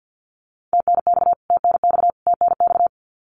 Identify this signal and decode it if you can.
Morse code, heard as a keyed tone, about 50 wpm, 700 Hz. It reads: TNX TNX TNX
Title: Abbreviations Repeated 3x - 50wpm